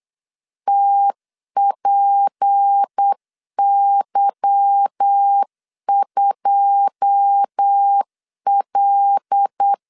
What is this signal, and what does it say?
Morse code, heard as a keyed tone, about 8 wpm, 800 Hz. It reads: TPY2L
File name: LEZIONI / 9 28 8wpm 3512